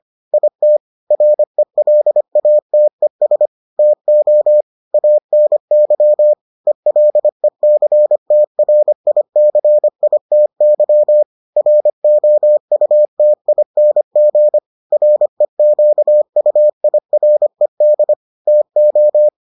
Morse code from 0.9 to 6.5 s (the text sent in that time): RELATES TO ANY